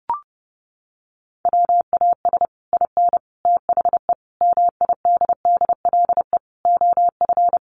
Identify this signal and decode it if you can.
Morse code, heard as a keyed tone, about 30 wpm, 700 Hz. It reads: WAS IN THE MIDDLE OF
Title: Sets of 5 Words Encore 30wpm